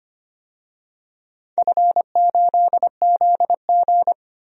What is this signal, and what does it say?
Morse code, heard as a keyed tone, about 25 wpm, 700 Hz. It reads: F8ZG
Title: DXCC-100 Call Signs - 4 Characters - 25wpm